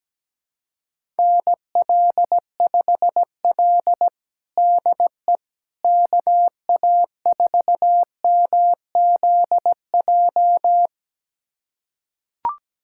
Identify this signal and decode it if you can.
Morse code, heard as a keyed tone, about 17 wpm, 700 Hz. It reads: NL5L DE KA4MZJ
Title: USA Callsign DE Callsign 17wpm